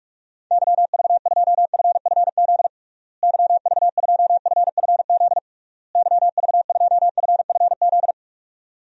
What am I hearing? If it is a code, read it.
Morse code, heard as a keyed tone, about 44 wpm, 700 Hz. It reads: YV2FFZ YV2FFZ YV2FFZ
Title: DXCC-100 Call Signs - 6 Characters - Repeated 3x - 45wpm